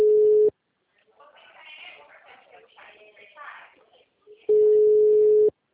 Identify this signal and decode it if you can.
Ringback tone (Europe, 425 Hz), with speech.